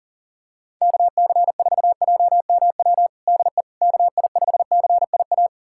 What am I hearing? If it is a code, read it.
Morse code, heard as a keyed tone, about 40 wpm, 700 Hz. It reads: KC4JMW DE KI5CIA
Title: USA Callsign DE Callsign 40wpm